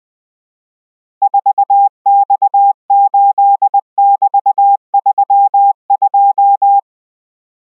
Morse code, heard as a keyed tone, about 20 wpm, 800 Hz. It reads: 4X8<BT>32